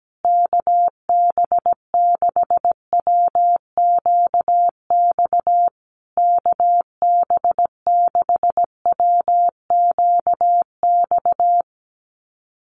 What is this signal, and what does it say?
Morse code, heard as a keyed tone, about 17 wpm, 700 Hz. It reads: KB6WQX KB6WQX
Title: USA Call Signs Repeated 3x 17wpm